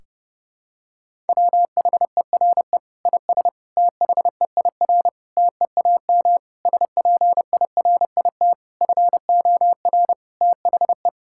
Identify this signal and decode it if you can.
Morse code, heard as a keyed tone, about 32 wpm, 700 Hz. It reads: WHERE IS THEIR TEAM SPIRIT FOR THE
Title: Sentences from Top 1600 Words 30wpm